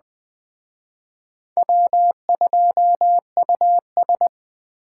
Morse code, heard as a keyed tone, about 21 wpm, 700 Hz. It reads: W2US
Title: Speed Racing - USA 1x2 Call Signs 17wpm